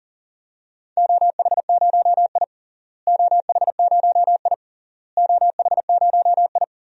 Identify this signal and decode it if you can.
Morse code, heard as a keyed tone, about 37 wpm, 700 Hz. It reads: OH0I OH0I OH0I
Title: DXCC-100 Call Signs - 4 Characters - Repeated 3x - 40wpm